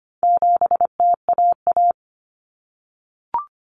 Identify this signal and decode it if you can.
Morse code, heard as a keyed tone, about 25 wpm, 700 Hz. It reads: 7TAA
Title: Common Cut Numbers - 4-Digit 25wpm